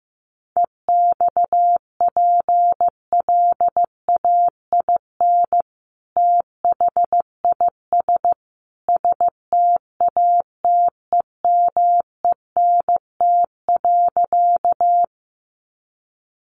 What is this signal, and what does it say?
Morse code, heard as a keyed tone, about 15 wpm, 700 Hz. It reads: EXPLAIN THIS STATEMENT.
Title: Sentences from Top 1200 Words 15wpm